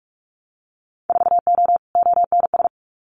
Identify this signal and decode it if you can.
Morse code, heard as a keyed tone, about 46 wpm, 700 Hz. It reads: 4O ONS
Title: Winter Field Day Exchanges 45wpm